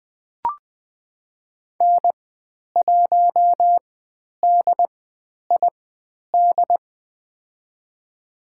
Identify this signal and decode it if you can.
Morse code, heard as a keyed tone, about 20 wpm, spaced as for 10 wpm, 700 Hz. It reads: N1DID
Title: 095 Lesson 16g - Letter c - Call Signs 20wpm